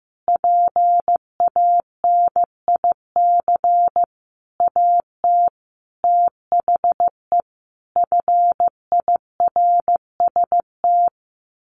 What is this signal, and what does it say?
Morse code, heard as a keyed tone, about 15 wpm, 700 Hz. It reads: PANIC AT THE FIRST
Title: A Sound Of Thunder 15wpm